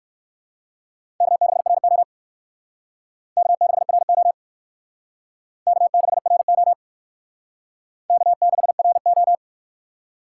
Morse code, heard as a keyed tone, about 55 wpm, 700 Hz. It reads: K6RO K6RO K6RO K6RO
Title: Speed Racing - USA 1x2 Call Signs 45wpm